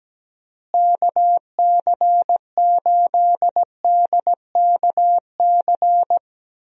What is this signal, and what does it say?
Morse code, heard as a keyed tone, about 17 wpm, 700 Hz. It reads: KC8DKC
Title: USA Call Signs - 6 Characters - Repeated 3x - 17wpm